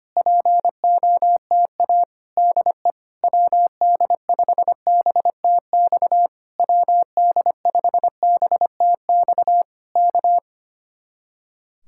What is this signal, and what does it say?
Morse code, heard as a keyed tone, about 24 wpm, 700 Hz. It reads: POTA DE WD5BTX WD5BTX K
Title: Speed-Racing Common QSO Phrases - 25wpm